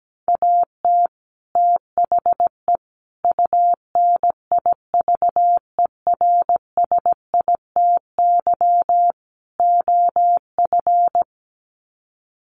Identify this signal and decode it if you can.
Morse code, heard as a keyed tone, about 17 wpm, 700 Hz. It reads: AT THE UNIVERSITY OF
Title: Sets of 4 Words 17wpm